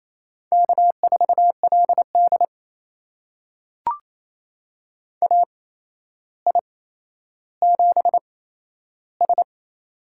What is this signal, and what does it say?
Morse code, heard as a keyed tone, about 28 wpm, spaced as for 12 wpm, 700 Hz. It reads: K4LD AI7S
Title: USA Call Signs - 4 Characters - 8x Character Spacing - 28wpm